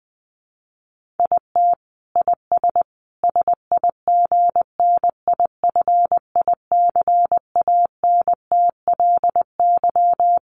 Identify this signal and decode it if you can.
Morse code, heard as a keyed tone, about 20 wpm, 700 Hz. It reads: IT IS SIGNIFICANTLY
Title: Sentences from Top 1500 Words 20wpm